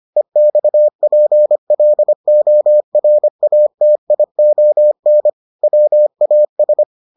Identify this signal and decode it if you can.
Morse code, heard as a keyed tone, about 25 wpm, 600 Hz. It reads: EXPLORATION WAS